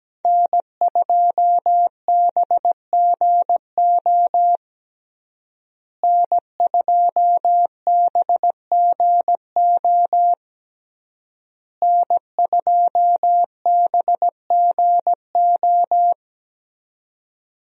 Morse code, heard as a keyed tone, about 17 wpm, 700 Hz. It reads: N2BGO N2BGO N2BGO
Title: USA Call Signs - 5 Characters - Repeated 3x - 17wpm